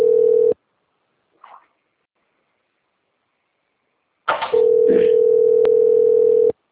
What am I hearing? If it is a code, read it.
Ringback tone (North America), 440 + 480 Hz.